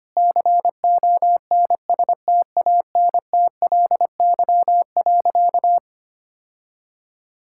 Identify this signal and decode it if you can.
Morse code, heard as a keyed tone, about 25 wpm, 700 Hz. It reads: CONSTANTLY.
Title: Sentences from Top 1600 Words 25wpm